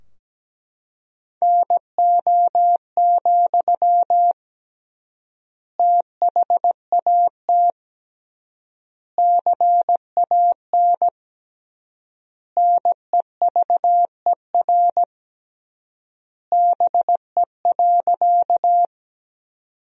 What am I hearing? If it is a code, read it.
Morse code, heard as a keyed tone, about 17 wpm, 700 Hz. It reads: NO, THAT CAN NEVER BE.
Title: 3x Word-Spacing - Sentences from Top 300 Words - 17wpm